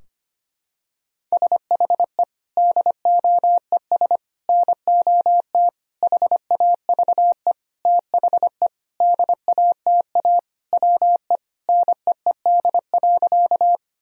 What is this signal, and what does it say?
Morse code, heard as a keyed tone, about 25 wpm, 700 Hz. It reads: SHE DOES NOT HAVE THE DATA WE NEED.
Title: Sentences from Top 1000 Words - Review Part 3 - 25wpm